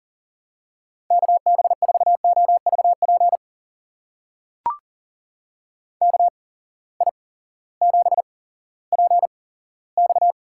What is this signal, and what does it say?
Morse code, heard as a keyed tone, about 40 wpm, spaced as for 18 wpm, 700 Hz. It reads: KB4OVP KI7PX
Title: USA Call Signs - 6 Characters - 8x Character Spacing - 40wpm